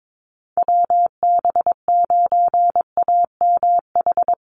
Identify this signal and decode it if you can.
Morse code, heard as a keyed tone, about 22 wpm, 700 Hz. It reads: WB9AMH
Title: USA Call Signs Repeated 3x 22wpm